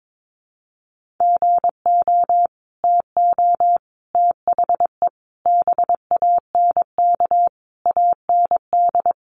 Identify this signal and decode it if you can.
Morse code, heard as a keyed tone, about 22 wpm, 700 Hz. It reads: GO TO THE BANK AND
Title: Sentences from Top 1500 Words 22wpm